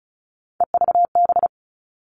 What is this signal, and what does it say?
Morse code, heard as a keyed tone, about 34 wpm, 700 Hz. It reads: EVB
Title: All Cut Numbers - 3-Digit 35wpm